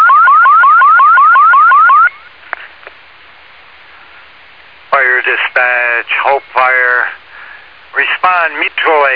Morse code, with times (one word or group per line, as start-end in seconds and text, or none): none